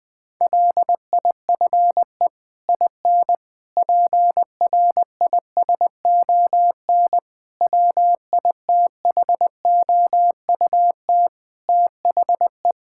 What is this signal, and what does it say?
Morse code, heard as a keyed tone, about 20 wpm, 700 Hz. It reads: LIFE IN PRISON WITHOUT THE
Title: Sets of 8 Words - 20wpm